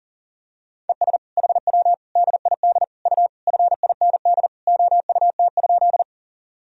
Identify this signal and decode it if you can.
Morse code, heard as a keyed tone, about 40 wpm, 700 Hz. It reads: ES HW DID U FIND OUT?
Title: Sentences from Top 100 Words and Abbreviations 40wpm